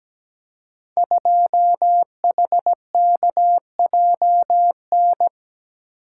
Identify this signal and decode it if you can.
Morse code, heard as a keyed tone, about 17 wpm, 700 Hz. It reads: 2HKJN
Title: Code Groups - 5 Letter:Numbers - 17wpm